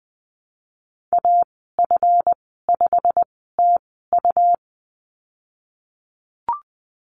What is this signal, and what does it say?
Morse code, heard as a keyed tone, about 20 wpm, spaced as for 14 wpm, 700 Hz. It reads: AF5TU
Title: 147 Lesson 24g - Letter y - Call Signs 20wpm